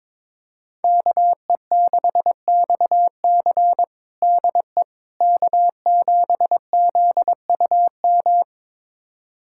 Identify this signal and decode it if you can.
Morse code, heard as a keyed tone, about 22 wpm, 700 Hz. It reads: KE6XC DE K7ZUM
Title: USA Callsign DE Callsign - 5 Characters - 22wpm